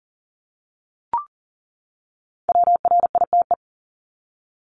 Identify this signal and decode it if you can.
Morse code, heard as a keyed tone, about 40 wpm, 700 Hz. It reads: WRITE
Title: ING Suffix - Top 100 Words with Base Words 40wpm